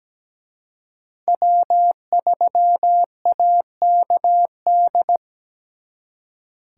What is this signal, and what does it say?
Morse code, heard as a keyed tone, about 20 wpm, 700 Hz. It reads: W3AKD
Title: Speed Racing - USA 1x3 Call Signs 17wpm